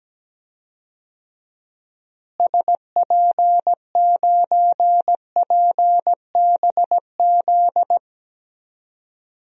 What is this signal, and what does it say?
Morse code, heard as a keyed tone, about 17 wpm, 700 Hz. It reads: SP9PBZ
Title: DXCC-100 Call Signs - 6 Characters - Repeated 2x - 17wpm